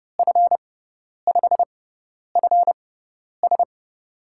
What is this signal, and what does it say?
Morse code, heard as a keyed tone, about 30 wpm, spaced as for 10 wpm, 700 Hz. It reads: F5FS
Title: USA Call Signs - 5 Characters - 6x Character Spacing - 30wpm